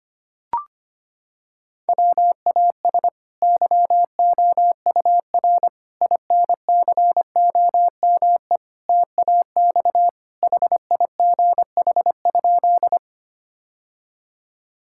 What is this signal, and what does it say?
Morse code, heard as a keyed tone, about 25 wpm, 700 Hz. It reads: WAS YOUR INCOME TAX HIGH?
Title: Sentences from Top 1000 Words - Review Part 5 - 25wpm